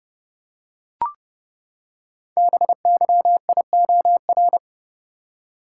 Morse code, heard as a keyed tone, about 28 wpm, 700 Hz. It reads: BYIOR
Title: Code Groups - 5 Letters - 30wpm